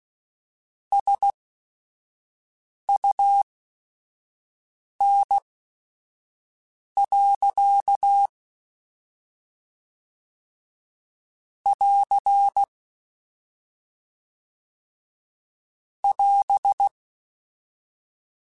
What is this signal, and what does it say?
Morse code, heard as a keyed tone, about 16 wpm, spaced as for 5 wpm, 800 Hz. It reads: SUN. <AR> <AS>
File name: Weather Sets / Weather Sets-mp3s / Weather 5 5 @5